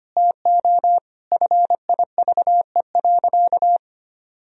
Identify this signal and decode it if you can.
Morse code, heard as a keyed tone, about 25 wpm, 700 Hz. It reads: TO FIVE.
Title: Sentences from Top 1000 Words - Review Part 2 - 25wpm